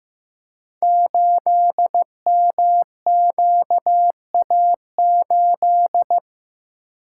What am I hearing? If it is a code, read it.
Morse code, heard as a keyed tone, about 15 wpm, 700 Hz. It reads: 8MQA8
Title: Code Groups - 5 Letter:Numbers - 15wpm